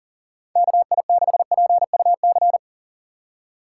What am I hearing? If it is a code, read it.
Morse code, heard as a keyed tone, about 37 wpm, 700 Hz. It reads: KI6PUC
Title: USA Call Signs - 6 Characters - 4x Character Spacing - 40wpm